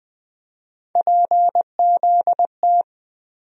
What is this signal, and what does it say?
Morse code, heard as a keyed tone, about 20 wpm, 700 Hz. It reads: PZT
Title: USA Call Sign Postfixes - 3 Letters 20wpm